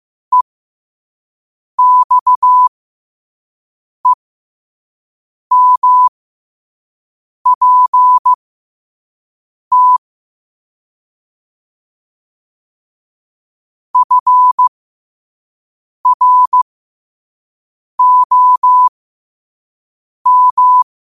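Morse code, heard as a keyed tone, about 15 wpm, spaced as for 5 wpm, 1000 Hz. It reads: EXEMPT FROM
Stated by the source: Morse code quote of the day sent at 5 WPM.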